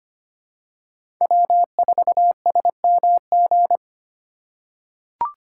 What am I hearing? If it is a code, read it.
Morse code, heard as a keyed tone, about 25 wpm, 700 Hz. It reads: W4SMG
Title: USA Call Signs - 5 Characters - 2x Character Spacing - 25wpm